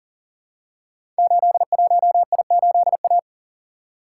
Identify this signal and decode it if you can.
Morse code, heard as a keyed tone, about 39 wpm, 700 Hz. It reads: 81I8A
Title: Code Groups - 5 Letter:Numbers - 40wpm